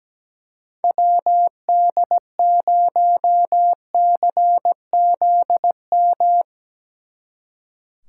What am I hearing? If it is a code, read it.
Morse code, heard as a keyed tone, about 19 wpm, 700 Hz. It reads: WD0CZM
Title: Speed Racing - USA 2x3 Call Signs 17wpm